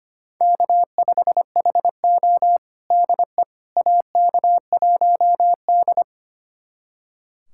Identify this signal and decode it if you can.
Morse code, heard as a keyed tone, about 25 wpm, 700 Hz. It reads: K5HO DE AK1D
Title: USA Callsign DE Callsign - 4 Characters - 25wpm